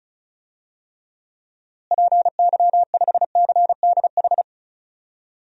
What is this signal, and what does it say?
Morse code, heard as a keyed tone, about 34 wpm, 700 Hz. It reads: PY5CDH
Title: DXCC-100 Call Signs - 6 Characters - Repeated 3x - 35wpm